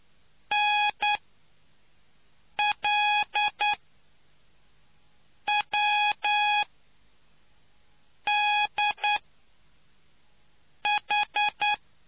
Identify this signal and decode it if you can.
Morse code, heard as a keyed tone, about 9 wpm, spaced as for 4 wpm, 830 Hz. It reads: NLWDH